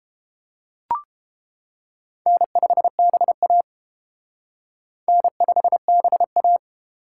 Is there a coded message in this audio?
Yes: Morse code from 2.3 to 6.6 s, heard as a keyed tone, about 29 wpm, 700 Hz. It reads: N5BA N5BA